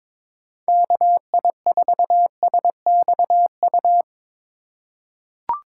Morse code, heard as a keyed tone, about 21 wpm, 700 Hz. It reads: KI4SXU
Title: USA Call Signs - 6 Characters - 3x Character Spacing - 22wpm